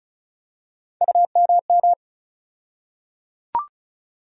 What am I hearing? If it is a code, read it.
Morse code, heard as a keyed tone, about 36 wpm, 700 Hz. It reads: UMM
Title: USA Call Sign Postfixes - 3 Letters 35wpm